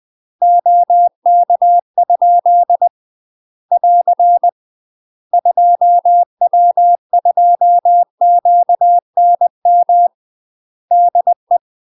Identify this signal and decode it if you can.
Morse code, heard as a keyed tone, about 20 wpm, 700 Hz. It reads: OK? <AR> 2W2QNM DE
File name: UK MASTER QSOs TRAINING / 20WPM / UK MASTER QSOs 0052